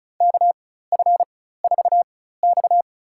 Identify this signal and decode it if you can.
Morse code, heard as a keyed tone, about 36 wpm, spaced as for 16 wpm, 700 Hz. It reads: KF4X
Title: USA Call Signs - 4 Characters - 4x Character Spacing - 35wpm